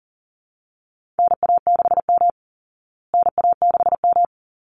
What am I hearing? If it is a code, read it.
Morse code, heard as a keyed tone, about 40 wpm, 700 Hz. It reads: NA6M NA6M
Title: Call Signs Repeated 3x - 40wpm